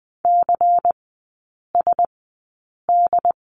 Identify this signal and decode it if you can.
Morse code, heard as a keyed tone, about 20 wpm, spaced as for 8 wpm, 700 Hz. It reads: CSD
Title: 133 Lesson 22e - Question Mark - Three Characters 20wpm